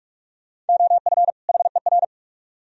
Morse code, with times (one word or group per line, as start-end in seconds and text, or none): 0.7-1.3 OF
1.5-2.0 HER